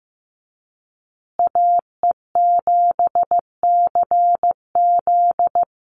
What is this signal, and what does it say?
Morse code, heard as a keyed tone, about 15 wpm, 700 Hz. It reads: AE7CZ
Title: USA Call Signs Repeated 3x 15wpm